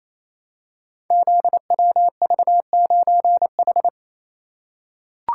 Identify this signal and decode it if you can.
Morse code, heard as a keyed tone, about 27 wpm, 700 Hz. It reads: ZWV9H
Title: Code Groups - 5 Letter:Numbers - 28wpm